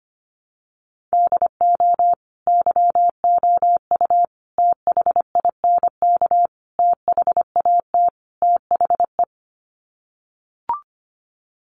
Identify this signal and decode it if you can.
Morse code, heard as a keyed tone, about 25 wpm, 700 Hz. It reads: DO YOU THINK THAT THE
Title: Sets of 5 Words Encore 25wpm